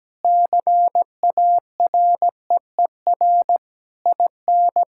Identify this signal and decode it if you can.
Morse code, heard as a keyed tone, about 17 wpm, 700 Hz. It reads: CAREER IN
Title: Sentences from Top 1000 Words - Review Part 6 - 17wpm